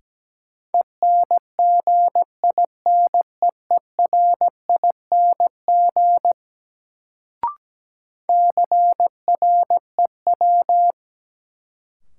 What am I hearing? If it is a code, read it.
Morse code, heard as a keyed tone, about 17 wpm, 700 Hz. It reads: ENGINEERING CREW
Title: Sentences from Top 1500 Words - New Word Drill Alt - 17wpm